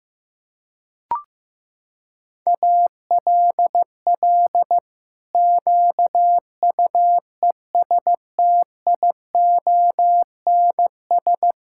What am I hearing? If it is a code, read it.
Morse code, heard as a keyed tone, about 14 wpm, 700 Hz. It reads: ALL QUESTIONS
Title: Pangram Challenge v2 - 15wpm